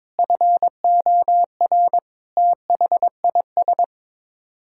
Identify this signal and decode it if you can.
Morse code, heard as a keyed tone, about 22 wpm, 700 Hz. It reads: FOR THIS
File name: practice / 22wpm / Sets of 8 Words - 22wpm